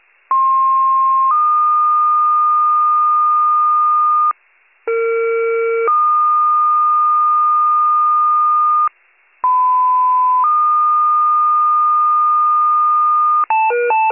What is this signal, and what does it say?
Two-tone page, heard as a one-second tone, then a three-second tone, about 1060 then 1220 Hz, about 460 then 1220 Hz, about 1010 then 1220 Hz. It is heard from 0.3 to 13.4 s.